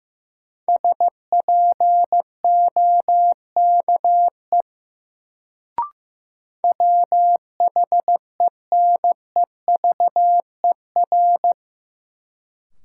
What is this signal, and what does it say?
Morse code, heard as a keyed tone, about 16 wpm, 700 Hz. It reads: SPOKE WHENEVER
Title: Sentences from Top 1200 Words - New Word Drill Alt - 15wpm